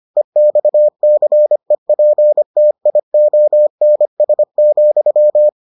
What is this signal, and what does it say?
Morse code, heard as a keyed tone, about 25 wpm, 600 Hz. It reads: EXCEPTIONS,